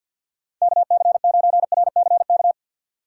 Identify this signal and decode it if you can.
Morse code, heard as a keyed tone, about 52 wpm, 700 Hz. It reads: KC9RCK
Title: USA Call Signs - 6 Characters - 6x Character Spacing - 50wpm